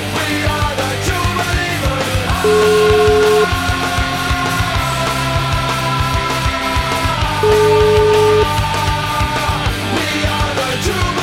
Ringback tone (Europe, 425 Hz), with noise.